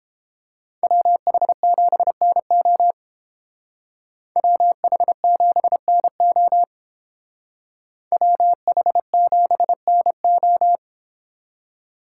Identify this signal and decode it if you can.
Morse code, heard as a keyed tone, about 30 wpm, 700 Hz. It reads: WH7NO WH7NO WH7NO